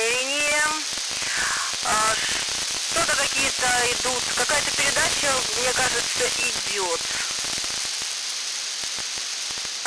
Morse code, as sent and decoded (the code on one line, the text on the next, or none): none